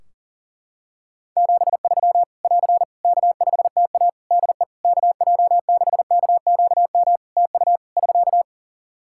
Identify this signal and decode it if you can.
Morse code, heard as a keyed tone, about 37 wpm, 700 Hz. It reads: 73 <AR> K5TA DE KJ6KQM TU <SK>
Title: Practice QSO - Exchange 4 - Ending - 40wpm